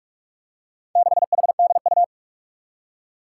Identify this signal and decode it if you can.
Morse code, heard as a keyed tone, about 41 wpm, 700 Hz. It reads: 6HDU
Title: Four Challenging Letter-Number Combinations 45wpm